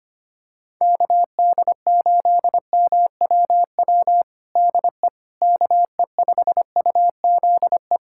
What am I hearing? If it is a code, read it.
Morse code, heard as a keyed tone, about 25 wpm, 700 Hz. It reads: KD8MWW DE KE5UZE
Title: USA Callsign DE Callsign - 6 Characters - 25wpm